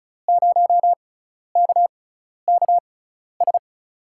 Morse code, heard as a keyed tone, about 36 wpm, spaced as for 12 wpm, 700 Hz. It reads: 0KKS
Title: USA Call Signs - 6 Characters - 6x Character Spacing - 35wpm